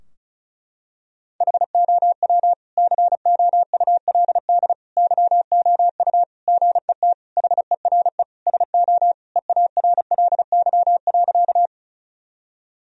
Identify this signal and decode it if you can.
Morse code, heard as a keyed tone, about 36 wpm, 700 Hz. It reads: HOW COULD YOU GET HERE SO EARLY.
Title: Sentences from Top 1000 Words - Review Part 2 - 35wpm